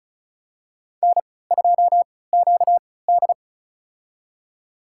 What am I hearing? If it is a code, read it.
Morse code, heard as a keyed tone, about 36 wpm, spaced as for 20 wpm, 700 Hz. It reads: N2QD
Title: USA Call Signs - 4 Characters - 3x Character Spacing - 35wpm